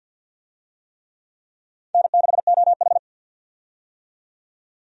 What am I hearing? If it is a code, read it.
Morse code, heard as a keyed tone, about 53 wpm, 700 Hz. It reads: N6OH
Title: USA Call Signs - 4 Characters - 50wpm